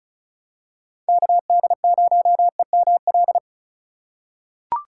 Morse code, heard as a keyed tone, about 36 wpm, 690 Hz. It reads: KD0EML
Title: USA Call Signs - 3x Character Spacing - 35wpm